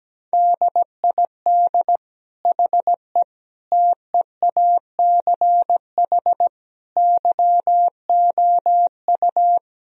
Morse code, heard as a keyed tone, about 17 wpm, 700 Hz. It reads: DID HE TEACH YOU
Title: Sentences from Top 1000 Words - Review Part 2 - 17wpm